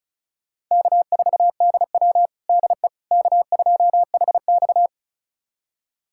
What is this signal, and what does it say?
Morse code, heard as a keyed tone, about 36 wpm, 700 Hz. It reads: K4DW DE K2HX
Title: USA Callsign DE Callsign - 4 Characters - 35wpm